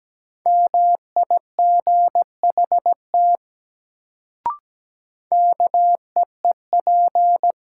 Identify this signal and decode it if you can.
Morse code, heard as a keyed tone, about 17 wpm, 700 Hz. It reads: MIGHT KEEP
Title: Sets of 2 Words - New Word Drill - 17wpm